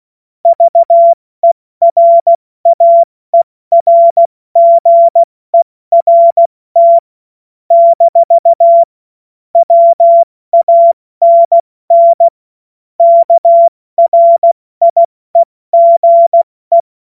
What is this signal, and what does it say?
Morse code, heard as a keyed tone, about 16 wpm, 680 Hz. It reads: VERAERGERT - WANN KRIEGE
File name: Morsekurs / Witze / Witz 27. April 2021, 70 F80